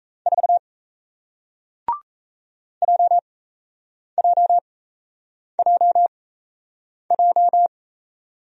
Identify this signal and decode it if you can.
Morse code, heard as a keyed tone, about 38 wpm, spaced as for 8 wpm, 700 Hz. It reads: 4 JJJJ